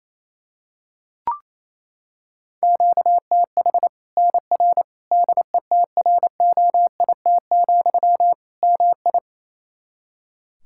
Morse code, heard as a keyed tone, about 27 wpm, 700 Hz. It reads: QTH NR DETROIT, MI
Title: Practice QSO - Exchange 1 - Beginning - 28wpm